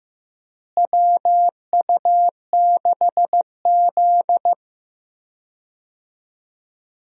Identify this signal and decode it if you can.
Morse code, heard as a keyed tone, about 15 wpm, 700 Hz. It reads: WU6Z
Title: USA Call Signs - 4 Characters - Repeated 2x - 15wpm